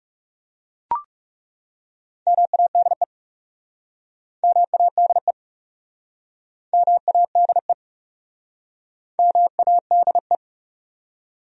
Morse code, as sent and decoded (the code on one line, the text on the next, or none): -- .- -.. / -- .- -.. / -- .- -.. / -- .- -.. .
MADE MADE MADE MADE